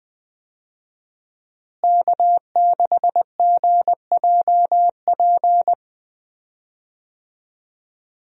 Morse code, heard as a keyed tone, about 20 wpm, 700 Hz. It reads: K6GJP
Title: USA Call Signs - 5 Characters - Repeated 2x - 20wpm